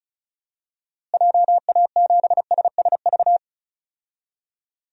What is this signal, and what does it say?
Morse code, heard as a keyed tone, about 36 wpm, 700 Hz. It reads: JA7SSV
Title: DXCC-100 Call Signs - 6 Characters - Repeated 3x - 35wpm